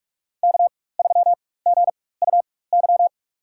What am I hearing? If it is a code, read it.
Morse code, heard as a keyed tone, about 46 wpm, spaced as for 21 wpm, 700 Hz. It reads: K3GUY
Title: USA Call Signs - 5 Characters - 4x Character Spacing - 45wpm